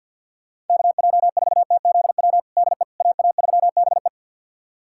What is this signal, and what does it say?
Morse code, heard as a keyed tone, about 51 wpm, 700 Hz. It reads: KJ4TZW DE AA3BE
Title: USA Callsign DE Callsign 50wpm